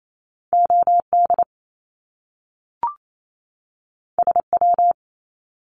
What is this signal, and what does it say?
Morse code, heard as a keyed tone, about 28 wpm, 700 Hz. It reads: OD SW
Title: Top Two Letter Combinations in Words 28wpm